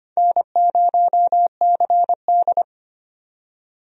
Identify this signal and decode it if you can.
Morse code, heard as a keyed tone, about 25 wpm, 700 Hz. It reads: N0CD
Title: USA Call Signs - 4 Characters - 25wpm